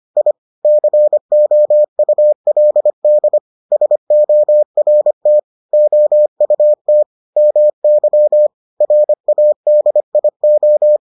Morse code, heard as a keyed tone, about 25 wpm, 600 Hz. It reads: I COULD SORT OUT MY RADIO